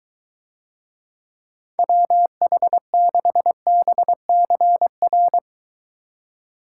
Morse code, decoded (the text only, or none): WH6BCR